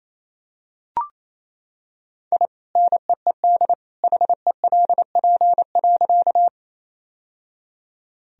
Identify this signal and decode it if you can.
Morse code, heard as a keyed tone, about 28 wpm, 700 Hz. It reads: I NEED HELP.
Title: Sentences from Top 1000 Words - Review Part 1 - 28wpm